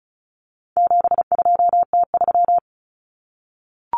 Morse code, heard as a keyed tone, about 34 wpm, 700 Hz. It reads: 72T3
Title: Common Cut Numbers - 4-Digit 35wpm